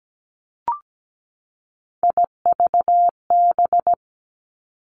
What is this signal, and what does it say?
Morse code, heard as a keyed tone, about 17 wpm, 700 Hz. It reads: IVB
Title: Three Challenging Letter-Number Combinations 17wpm